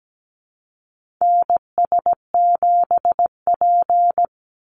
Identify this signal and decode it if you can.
Morse code, heard as a keyed tone, about 17 wpm, 700 Hz. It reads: NS7P
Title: NA SOTA Activator Callsigns 17wpm